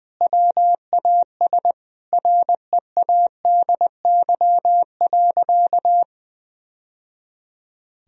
Morse code, heard as a keyed tone, about 20 wpm, 700 Hz. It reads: WAS READY.
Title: Sentences from Top 1000 Words - Review Part 3 - 20wpm